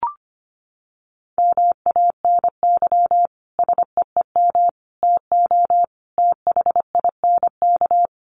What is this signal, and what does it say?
Morse code, heard as a keyed tone, about 25 wpm, 700 Hz. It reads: MANY SEEM TO THINK
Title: Sentences from Top 1400 Words 25wpm